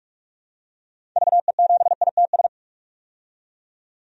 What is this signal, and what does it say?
Morse code, heard as a keyed tone, about 44 wpm, 700 Hz. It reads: VE7ITS
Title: DXCC-100 Call Signs - 6 Characters - 45wpm